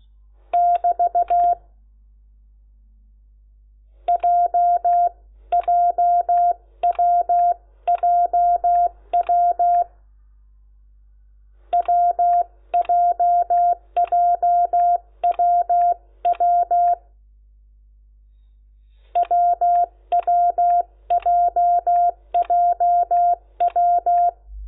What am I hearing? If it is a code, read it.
Morse code, heard as a keyed tone, about 16 wpm, 700 Hz. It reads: <BT> JJWJW WJJWW WWJJW
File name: Lecciones / CLASE Nº10